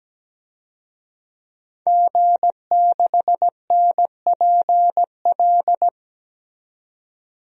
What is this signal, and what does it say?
Morse code, heard as a keyed tone, about 17 wpm, 700 Hz. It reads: G6NPL
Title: DXCC-100 Call Signs - 5 Characters - 17wpm